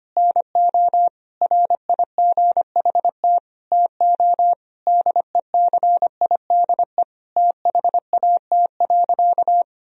Morse code, heard as a keyed tone, about 25 wpm, 700 Hz. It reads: NO RIGHT TO DECIDE THAT.
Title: Sentences from Top 1000 Words - Review Part 3 - 25wpm